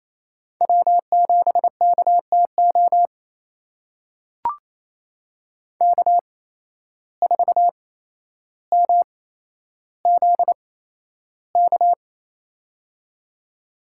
Morse code, heard as a keyed tone, about 28 wpm, spaced as for 12 wpm, 700 Hz. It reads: W7KTO K4MZK
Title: USA Call Signs - 5 Characters - 8x Character Spacing - 28wpm